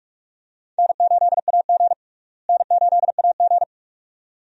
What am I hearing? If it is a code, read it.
Morse code, heard as a keyed tone, about 41 wpm, 700 Hz. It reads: N8AG N8AG
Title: USA Call Signs - 4 Characters - Repeated 2x - 45wpm